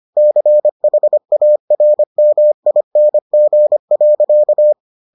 Morse code, heard as a keyed tone, about 25 wpm, 600 Hz. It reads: CHARMING.